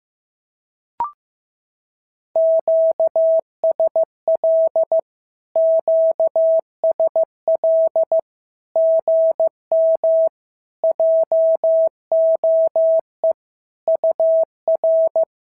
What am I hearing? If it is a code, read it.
Morse code, heard as a keyed tone, about 15 wpm, 650 Hz. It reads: QSL QSL GM JOE UR
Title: SOTA QSO Practice 15wpm